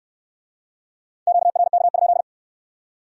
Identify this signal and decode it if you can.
Morse code, heard as a keyed tone, about 66 wpm, 700 Hz. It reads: QRK?